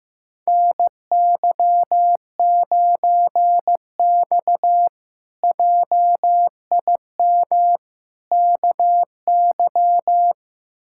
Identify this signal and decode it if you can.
Morse code, heard as a keyed tone, about 15 wpm, 700 Hz. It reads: NY9X JIM KY
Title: K1USN SST QSO Practice - 15wpm